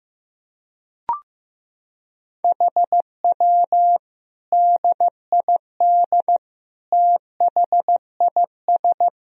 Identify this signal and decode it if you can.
Morse code, heard as a keyed tone, about 16 wpm, 700 Hz. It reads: HW DID THIS
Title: Sentences from Top 400 Words and Abbreviations 15wpm